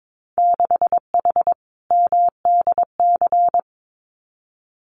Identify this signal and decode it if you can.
Morse code, heard as a keyed tone, about 22 wpm, 700 Hz. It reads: 6H MDC
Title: Winter Field Day Exchanges Encore 22wpm